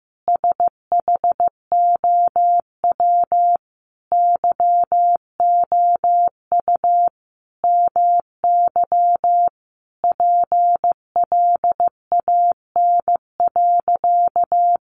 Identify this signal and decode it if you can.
Morse code, heard as a keyed tone, about 15 wpm, 700 Hz. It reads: SHOW YOU MY PLAN.
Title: Sentences from Top 500 Words 15wpm